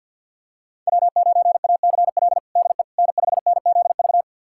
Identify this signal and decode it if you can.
Morse code, heard as a keyed tone, about 52 wpm, 700 Hz. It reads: W9ACL DE N5NZV
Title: USA Callsign DE Callsign - 5 Characters - 50wpm